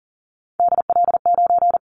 Morse code, heard as a keyed tone, about 40 wpm, 700 Hz. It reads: DL9
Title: Callsign Prefixes for Top 25 DXCC - Repeated 3x - 40wpm